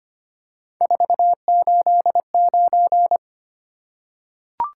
Morse code, heard as a keyed tone, about 25 wpm, 700 Hz. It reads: 489
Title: Top 180 QSO Elements 25wpm